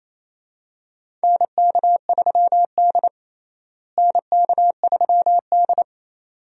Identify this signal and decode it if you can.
Morse code, heard as a keyed tone, about 28 wpm, 700 Hz. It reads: NK3D NK3D
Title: USA Call Signs - 4 Characters - Repeated 2x - 28wpm